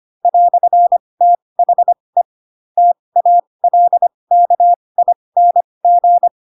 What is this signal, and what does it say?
Morse code, heard as a keyed tone, about 25 wpm, spaced as for 20 wpm, 700 Hz. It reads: "THE TALKING